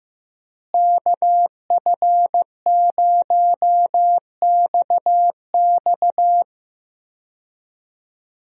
Morse code, heard as a keyed tone, about 16 wpm, 700 Hz. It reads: KF0XX
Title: USA Call Signs - 5 Characters - Repeated 2x - 15wpm